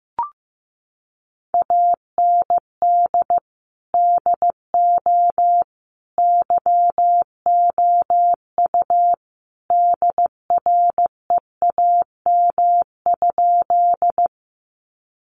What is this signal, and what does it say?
Morse code, heard as a keyed tone, about 15 wpm, 700 Hz. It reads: AND DO YOU DREAM?
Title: Sentences from Top 1500 Words 15wpm